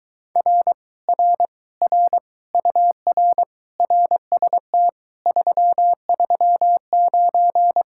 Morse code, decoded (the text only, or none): R R R UR RST 339